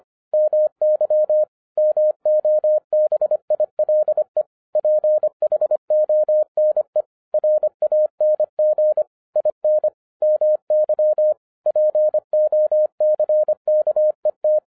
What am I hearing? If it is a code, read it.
Morse code, heard as a keyed tone, about 25 wpm, 610 Hz. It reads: MY MOBILE PHONE RANG IN MY POCKET
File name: 2011-07-mp3 / n34 THE 3.11 2 25wpm